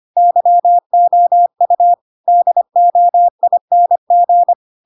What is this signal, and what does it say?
Morse code, heard as a keyed tone, about 25 wpm, 700 Hz. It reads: YOU DOING
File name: Pinocchio / Pinocchio ch2 25 25 ik1hge